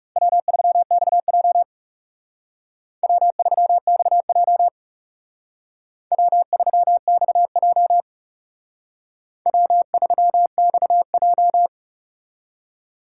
Morse code, heard as a keyed tone, about 39 wpm, 700 Hz. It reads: W3XJ W3XJ W3XJ W3XJ